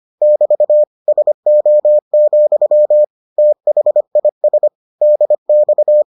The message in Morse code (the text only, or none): <BT> SO, THIS DX